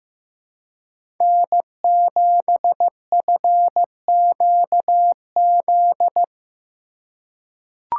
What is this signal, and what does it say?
Morse code, heard as a keyed tone, about 15 wpm, 700 Hz. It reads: N7FQZ
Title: USA Call Signs - 5 Characters - 4x Character Spacing - 15wpm